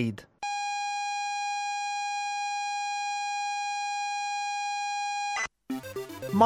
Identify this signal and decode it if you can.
ZX Spectrum tape loading, with speech.